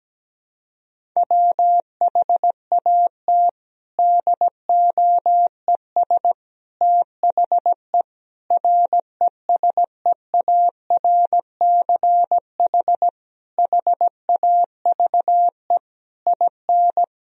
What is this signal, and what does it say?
Morse code, heard as a keyed tone, about 17 wpm, 700 Hz. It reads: WHAT DOES THE RESEARCH HAVE IN
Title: Sentences from Top 1000 Words - Review Part 4 - 17wpm